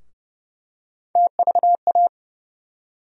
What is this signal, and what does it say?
Morse code, heard as a keyed tone, about 30 wpm, 700 Hz. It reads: TVA
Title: USA Call Sign Postfixes - 3 Letters 30wpm